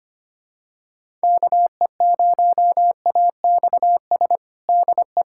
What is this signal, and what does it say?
Morse code, heard as a keyed tone, about 25 wpm, 700 Hz. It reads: KE0AXS DE
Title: USA Callsign DE Callsign 25wpm